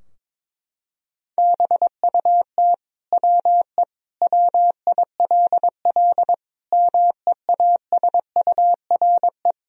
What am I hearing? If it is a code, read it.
Morse code, heard as a keyed tone, about 22 wpm, 700 Hz. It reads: BUT WE WILL MEASURE
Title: Sentences from Top 1000 Words - Review Part 2 - 22wpm